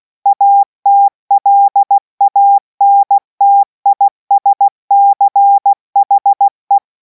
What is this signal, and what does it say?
Morse code, heard as a keyed tone, about 16 wpm, 800 Hz. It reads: ATLANTISCHE